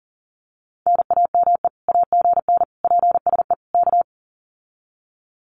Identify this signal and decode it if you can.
Morse code, heard as a keyed tone, about 40 wpm, 700 Hz. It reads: NAME AGN PSE K
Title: Common QSO Phrases Encore 40wpm